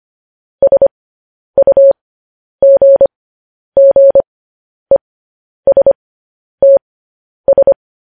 Morse code, heard as a keyed tone, about 25 wpm, spaced as for 10 wpm, 550 Hz. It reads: SUGGESTS
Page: news-headlines-in-morse-code-at-10-wpm